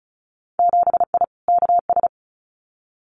Morse code, heard as a keyed tone, about 36 wpm, 700 Hz. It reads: 7I KS
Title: Winter Field Day Exchanges Encore 35wpm